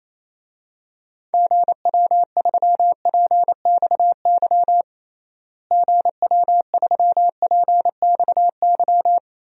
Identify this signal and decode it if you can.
Morse code, heard as a keyed tone, about 28 wpm, 700 Hz. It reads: GW3PXY GW3PXY
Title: DXCC-100 Call Signs - 6 Characters - Repeated 3x - 28wpm